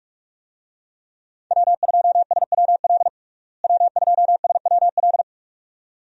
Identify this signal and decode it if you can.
Morse code, heard as a keyed tone, about 46 wpm, 700 Hz. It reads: W2SWL W2SWL
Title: USA Call Signs - 5 Characters - Repeated 2x - 45wpm